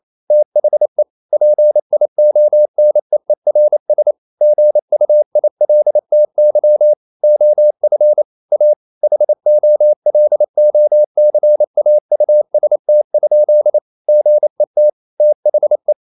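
Morse code, heard as a keyed tone, about 28 wpm, 600 Hz. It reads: THE PIONEERS GUILTY OF A HOLOCAUST? GET THE
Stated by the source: Morse code practice at 28 wpm, text of Ken Follett's novel Never in English, from CW med Gnister.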